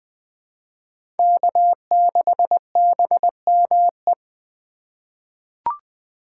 Morse code, heard as a keyed tone, about 21 wpm, 700 Hz. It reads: K6BME
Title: USA Call Signs - 5 Characters - 2x Character Spacing - 20wpm